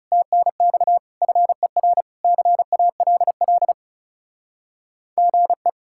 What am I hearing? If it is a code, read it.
Morse code, heard as a keyed tone, about 40 wpm, 700 Hz. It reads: TNX FER CALL GE